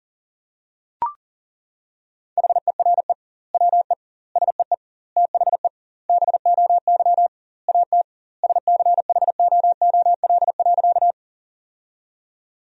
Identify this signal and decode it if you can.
Morse code, heard as a keyed tone, about 37 wpm, 700 Hz. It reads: HERE WE SEE THE BOY AT SCHOOL.
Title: 2x Word-Spacing - Sentences from Top 300 Words - 40wpm